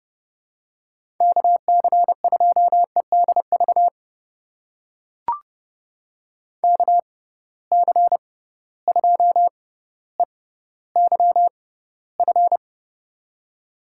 Morse code, heard as a keyed tone, about 30 wpm, spaced as for 15 wpm, 700 Hz. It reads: KC2EDV KC2EYF
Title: USA Call Signs - 6 Characters - 6x Character Spacing - 30wpm